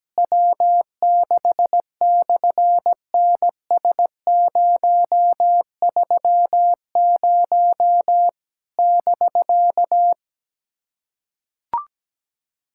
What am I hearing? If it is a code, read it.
Morse code, heard as a keyed tone, about 17 wpm, 700 Hz. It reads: W6/NS030 <BK>
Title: SOTA QSO Practice 17wpm